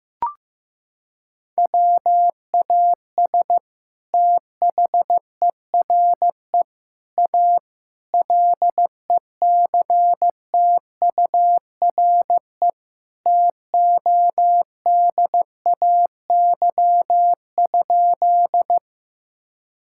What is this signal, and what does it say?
Morse code, heard as a keyed tone, about 15 wpm, 700 Hz. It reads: WAS THERE A LECTURE TODAY?
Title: Sentences from Top 1600 Words 15wpm